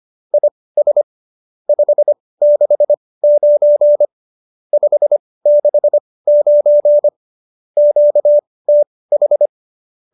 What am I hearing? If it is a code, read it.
Morse code, heard as a keyed tone, about 25 wpm, spaced as for 18 wpm, 590 Hz. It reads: IS 569 569 QTH
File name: qso208b 18